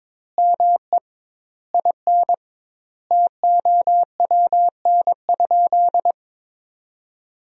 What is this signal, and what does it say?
Morse code, heard as a keyed tone, about 22 wpm, 700 Hz. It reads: ME IN TOWN?
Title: 2x Word-Spacing - Sentences from Top 500 Words - 22wpm